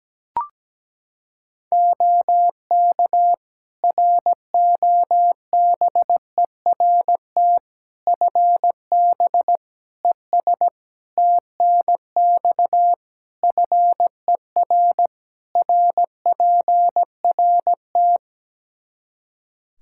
Morse code, heard as a keyed tone, about 17 wpm, 700 Hz. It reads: OK ROBERT FB ES TNX FER RPRT
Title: K1USN SST QSO Practice 17wpm